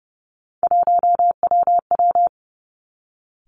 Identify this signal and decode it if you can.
Morse code, heard as a keyed tone, about 30 wpm, 700 Hz. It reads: 1WW
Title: Three Challenging Letter-Number Combinations 30wpm